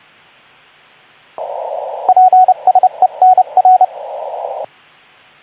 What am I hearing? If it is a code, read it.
Morse code, heard as a keyed tone, about 29 wpm, 740 Hz, with radio noise. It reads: PSE NR
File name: Archiv / CUC189 / ok2zcq cuc 189